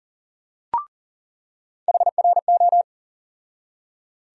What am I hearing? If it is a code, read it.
Morse code, heard as a keyed tone, about 40 wpm, 700 Hz. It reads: HRO
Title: USA Call Sign Postfixes - 3 Letters 40wpm